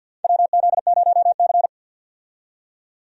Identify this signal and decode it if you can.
Morse code, heard as a keyed tone, about 51 wpm, 700 Hz. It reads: WZ0C
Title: USA Call Signs - 4 Characters - 50wpm